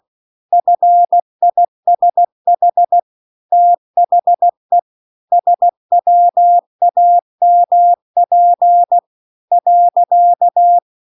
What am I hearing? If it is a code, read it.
Morse code, heard as a keyed tone, about 16 wpm, 700 Hz. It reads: FISH THE SWAMP .